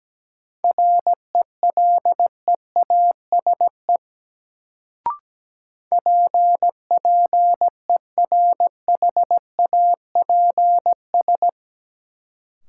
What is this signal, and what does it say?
Morse code, heard as a keyed tone, about 17 wpm, 700 Hz. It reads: RELEASE PPERHAPS
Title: Sentences from Top 1100 Words - New Word Drill - 17wpm